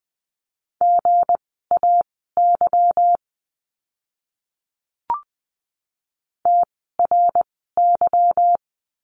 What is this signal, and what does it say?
Morse code, heard as a keyed tone, about 19 wpm, spaced as for 14 wpm, 700 Hz. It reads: GAY TRY
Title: 160 Lesson 26f - Letter g - Words 20wpm